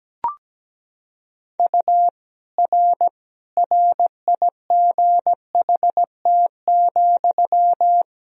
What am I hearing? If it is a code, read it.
Morse code, heard as a keyed tone, about 17 wpm, 700 Hz. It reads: U R RIGHT,
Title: Sentences from Top 300 Words and Abbreviations 17wpm